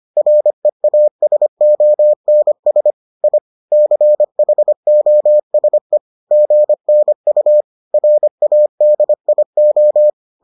Morse code, heard as a keyed tone, about 25 wpm, 600 Hz. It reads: REASONS I CHOSE GNU RADIO